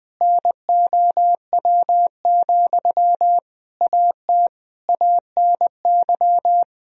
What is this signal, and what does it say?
Morse code, heard as a keyed tone, about 20 wpm, 700 Hz. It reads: NOW, AT ANY
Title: Sentences from Top 1000 Words - Review Part 4 - 20wpm